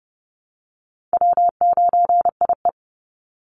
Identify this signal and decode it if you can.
Morse code, heard as a keyed tone, about 30 wpm, 700 Hz. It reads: W9IE
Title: Easy Call Signs Repeated 3x - 30wpm